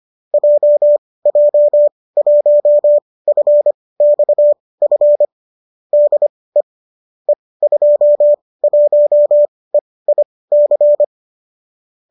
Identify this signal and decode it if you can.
Morse code, heard as a keyed tone, about 25 wpm, spaced as for 18 wpm, 570 Hz. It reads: JJ1FXF DE E21EIC